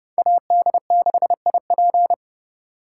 Morse code, heard as a keyed tone, about 30 wpm, 700 Hz. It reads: AD6IP